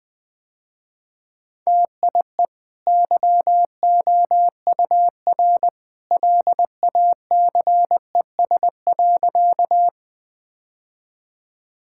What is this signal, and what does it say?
Morse code, heard as a keyed tone, about 21 wpm, 700 Hz. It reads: TIE YOUR LACES.
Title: Easy Sentences with Hints - 20wpm